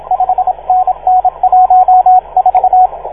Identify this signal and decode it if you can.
Morse code, heard as a keyed tone, about 29 wpm, 690 Hz, with radio noise. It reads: 5NN 14
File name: 1esempio-telegrafia